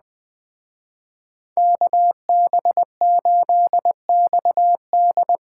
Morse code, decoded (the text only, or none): KB8XD